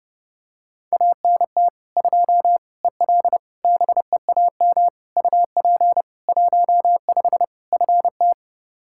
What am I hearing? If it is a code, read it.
Morse code, heard as a keyed tone, about 30 wpm, 700 Hz. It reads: ANT 2 EL BEAM UP 15 FT
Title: Practice QSO - Exchange 2 - Info - 30wpm